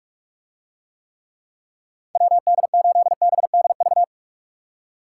Morse code, heard as a keyed tone, about 48 wpm, 700 Hz. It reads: WD8BDV